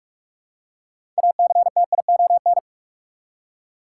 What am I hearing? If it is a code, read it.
Morse code, heard as a keyed tone, about 45 wpm, 700 Hz. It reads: ACTION
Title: Six Letter Words - 45wpm